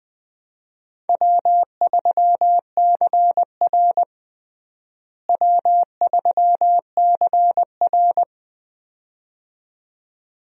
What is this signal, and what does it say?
Morse code, heard as a keyed tone, about 20 wpm, 700 Hz. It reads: W3CR W3CR
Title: USA Call Signs - 4 Characters - Repeated 2x - 20wpm